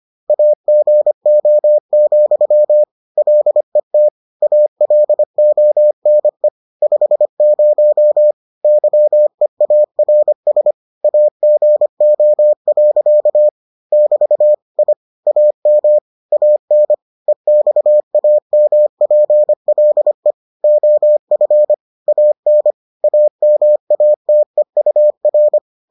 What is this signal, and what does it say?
Morse code, heard as a keyed tone, about 25 wpm, 600 Hz. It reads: AGO, LET ALONE 50 YEARS AGO. <BT> I AM AN EXAMPLE OF AN AMATEUR